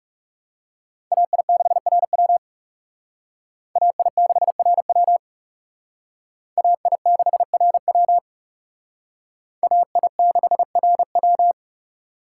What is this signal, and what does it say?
Morse code, heard as a keyed tone, about 37 wpm, 700 Hz. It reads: AI6RW AI6RW AI6RW AI6RW
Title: Speed Racing - USA Call Signs 30wpm